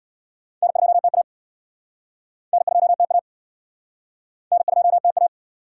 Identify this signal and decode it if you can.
Morse code, heard as a keyed tone, about 70 wpm, 700 Hz. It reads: N2TA N2TA N2TA